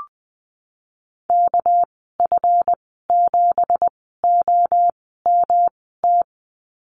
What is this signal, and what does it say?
Morse code, heard as a keyed tone, about 20 wpm, spaced as for 15 wpm, 700 Hz. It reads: KF7OMT
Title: 231 Lesson 37g - Letter x - Call Signs 20wpm